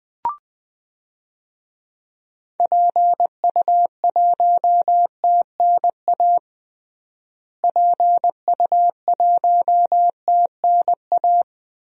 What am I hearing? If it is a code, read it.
Morse code, heard as a keyed tone, about 20 wpm, 700 Hz. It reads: PU1TNA PU1TNA
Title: DXCC-100 Call Signs - 6 Characters - Repeated 2x - 20wpm